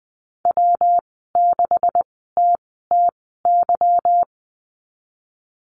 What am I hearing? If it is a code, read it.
Morse code, heard as a keyed tone, about 20 wpm, spaced as for 14 wpm, 700 Hz. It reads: W6TTY
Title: 147 Lesson 24g - Letter y - Call Signs 20wpm